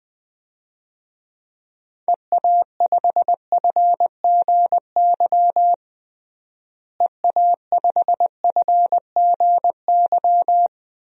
Morse code, heard as a keyed tone, about 20 wpm, 700 Hz. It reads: EA5FGY EA5FGY
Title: DXCC-100 Call Signs - 6 Characters - Repeated 2x - 20wpm